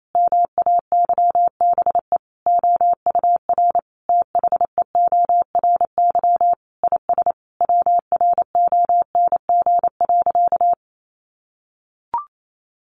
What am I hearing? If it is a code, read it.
Morse code, heard as a keyed tone, about 28 wpm, 700 Hz. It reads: MAYBE OUR THEORY IS WRONG.
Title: A Sound Of Thunder 28wpm